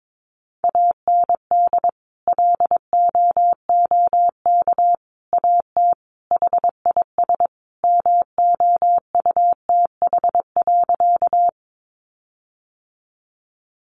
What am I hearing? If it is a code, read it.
Morse code, heard as a keyed tone, about 22 wpm, 700 Hz. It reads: AND LOOK AT HIS MOUTH.
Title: Sentences from Top 1500 Words 22wpm